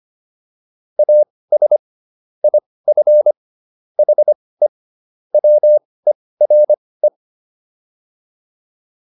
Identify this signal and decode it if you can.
Morse code, heard as a keyed tone, about 25 wpm, spaced as for 18 wpm, 600 Hz. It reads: AS IF HE WERE